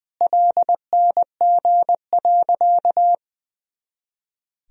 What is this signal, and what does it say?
Morse code, heard as a keyed tone, about 21 wpm, 700 Hz. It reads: LNG.
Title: Sentences from Top 100 Words and Abbreviations 20wpm